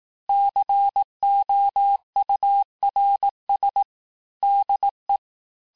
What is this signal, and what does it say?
Morse code, heard as a keyed tone, about 18 wpm, 800 Hz. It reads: COURS DE
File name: cours-du-lundi files / cours-1594--FIN-de-QSO-vitesse-17mpm